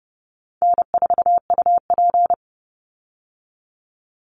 Morse code, heard as a keyed tone, about 30 wpm, 700 Hz. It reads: N4UP
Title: Call Signs - 30wpm